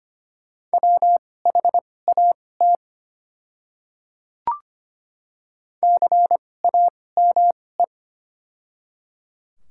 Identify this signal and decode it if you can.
Morse code, heard as a keyed tone, about 25 wpm, spaced as for 18 wpm, 700 Hz. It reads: WHAT CAME
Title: 2x Character Spacing - Top 200 Words - 25wpm